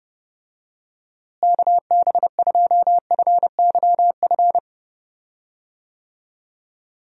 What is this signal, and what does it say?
Morse code, heard as a keyed tone, about 30 wpm, 700 Hz. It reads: KB2FYF
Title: USA Call Signs - 6 Characters - Repeated 3x - 30wpm